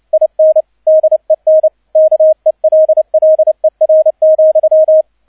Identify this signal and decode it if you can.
Morse code, heard as a keyed tone, about 29 wpm, 630 Hz, with radio noise. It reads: IN DEN KELLER,
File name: 002 Zweiter Streich 30wpm